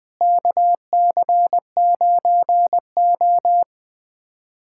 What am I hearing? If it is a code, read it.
Morse code, heard as a keyed tone, about 21 wpm, 700 Hz. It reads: KC9O
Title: USA Callsign DE Callsign - 4 Characters - 20wpm